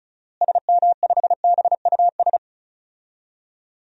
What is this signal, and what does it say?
Morse code, heard as a keyed tone, about 36 wpm, 700 Hz. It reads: SM5BUS
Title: DXCC-100 Call Signs - 6 Characters - Repeated 2x - 35wpm